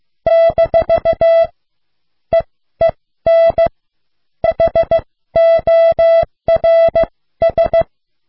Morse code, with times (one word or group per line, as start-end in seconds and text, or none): 0.3-1.4 -
2.3-3.7 EEN
4.4-7.8 HORS